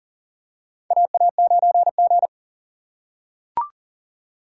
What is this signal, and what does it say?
Morse code, heard as a keyed tone, about 40 wpm, 700 Hz. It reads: AA9G
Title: USA Call Signs - 4 Characters - 6x Character Spacing - 40wpm